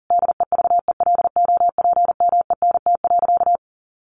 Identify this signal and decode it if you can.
Morse code, heard as a keyed tone, about 40 wpm, 700 Hz. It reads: DEVELOPMENT.
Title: Sentences from Top 1400 Words 40wpm